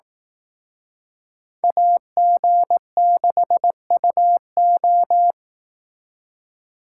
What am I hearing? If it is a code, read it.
Morse code, heard as a keyed tone, about 18 wpm, 700 Hz. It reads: AG6UO
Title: Speed Racing - USA 2x2 Call Signs 15wpm